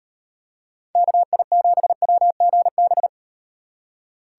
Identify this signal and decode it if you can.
Morse code, heard as a keyed tone, about 36 wpm, 700 Hz. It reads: KI7WGB